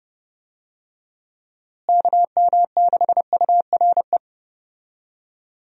Morse code, heard as a keyed tone, about 32 wpm, 700 Hz. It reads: KM6URE
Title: USA Call Signs - 6 Characters - Repeated 3x - 30wpm